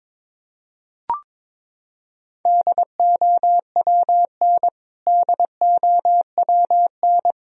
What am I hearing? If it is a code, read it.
Morse code, heard as a keyed tone, about 22 wpm, 700 Hz. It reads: DOWN DOWN
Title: Top 100 Words Repeated 22wpm